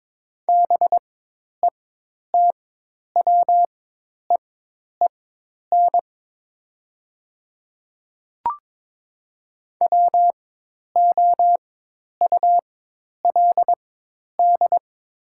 Morse code, heard as a keyed tone, about 22 wpm, spaced as for 10 wpm, 700 Hz. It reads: BETWEEN WOULD
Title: 4x Character Spacing - Top 300 Words - 22wpm